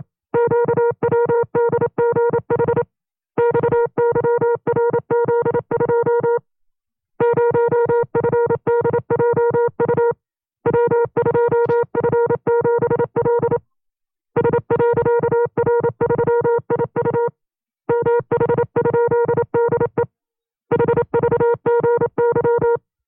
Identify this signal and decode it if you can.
Morse code, heard as a keyed tone, about 27 wpm, 470 Hz. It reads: QWDGH XYRZ2 0FDJU W2F7L S.R3IU MH?DE HVGY
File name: FreeFiles / CW-practice / 25WPM-side1